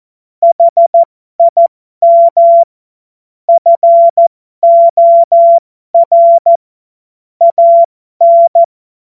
Morse code, heard as a keyed tone, about 14 wpm, spaced as for 12 wpm, 680 Hz. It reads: HIM FOR AN
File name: Morsekurs / Witze